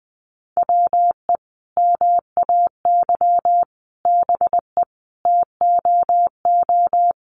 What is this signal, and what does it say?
Morse code, heard as a keyed tone, about 19 wpm, 700 Hz. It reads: WE MAY BE TOO
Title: Sentences from Top 300 Words 20wpm